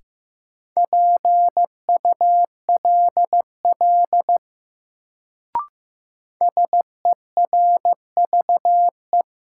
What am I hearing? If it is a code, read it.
Morse code, heard as a keyed tone, about 15 wpm, 700 Hz. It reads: PULL SERVE
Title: Sentences from Top 400 Words - New Word Drill - 15wpm